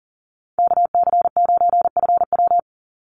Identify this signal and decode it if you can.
Morse code, heard as a keyed tone, about 40 wpm, 700 Hz. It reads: KC9FW
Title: USA Call Signs 40wpm